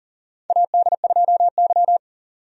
Morse code, heard as a keyed tone, about 40 wpm, 700 Hz. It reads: AD2Y
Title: USA Call Signs - 4 Characters - 40wpm